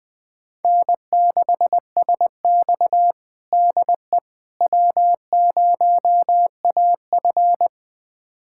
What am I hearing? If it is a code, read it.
Morse code, heard as a keyed tone, about 21 wpm, 700 Hz. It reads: N6SX DE W0AF
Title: USA Callsign DE Callsign - 4 Characters - 20wpm